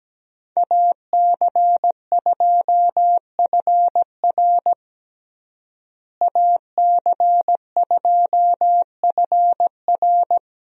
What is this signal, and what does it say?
Morse code, heard as a keyed tone, about 17 wpm, 700 Hz. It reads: AC2FR AC2FR
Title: USA Call Signs - 5 Characters - Repeated 3x - 17wpm